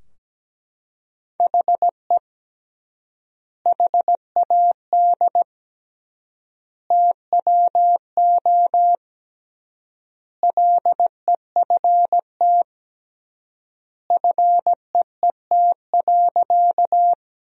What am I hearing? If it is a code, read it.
Morse code, heard as a keyed tone, about 17 wpm, 700 Hz. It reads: HE HAD TWO LEFT FEET.
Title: 3x Word-Spacing - Sentences from Top 300 Words - 17wpm